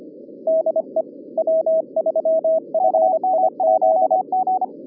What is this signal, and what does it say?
Morse code, heard as a keyed tone, about 25 wpm, 640 Hz, with radio noise. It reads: DE W3FIZ